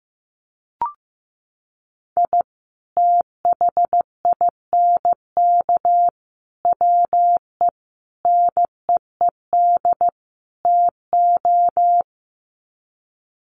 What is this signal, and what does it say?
Morse code, heard as a keyed tone, about 15 wpm, 700 Hz. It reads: I THINK WE NEED TO
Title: Sets of 5 Words 15wpm